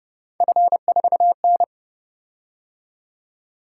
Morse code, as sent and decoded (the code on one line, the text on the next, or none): ..-. ....- -.
F4N